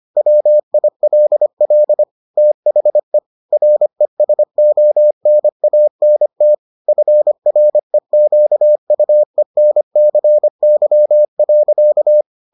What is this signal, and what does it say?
Morse code, heard as a keyed tone, about 25 wpm, 600 Hz. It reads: WILL THE RESONANT FREQUENCY.